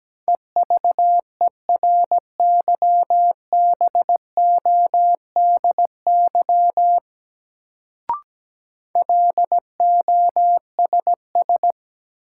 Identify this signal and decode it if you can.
Morse code, heard as a keyed tone, about 17 wpm, 700 Hz. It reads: EVERYBODY LOSS
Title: Sentences from Top 1400 Words - New Word Drill Alt - 17wpm